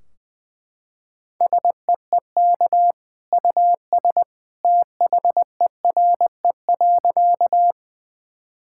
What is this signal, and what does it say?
Morse code, heard as a keyed tone, about 20 wpm, 700 Hz. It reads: SEEK US THERE.
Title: Sentences from Top 1600 Words 20wpm